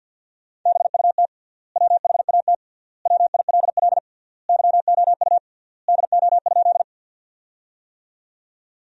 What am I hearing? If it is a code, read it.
Morse code, heard as a keyed tone, about 50 wpm, 700 Hz. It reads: BUT WHAT WILL YOU DO?
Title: 3x Word-Spacing - Sentences from Top 100 Words - 50wpm